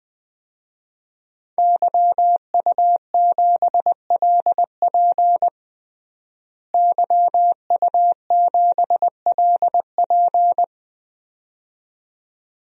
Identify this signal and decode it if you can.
Morse code, heard as a keyed tone, about 20 wpm, 700 Hz. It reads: YU7LP YU7LP
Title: DXCC-100 Call Signs - 5 Characters - Repeated 2x - 20wpm